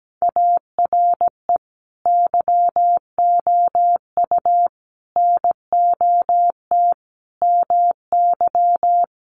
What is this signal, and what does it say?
Morse code, heard as a keyed tone, about 17 wpm, 700 Hz. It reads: ARE YOU NOT MY
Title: Sentences from Top 1300 Words 17wpm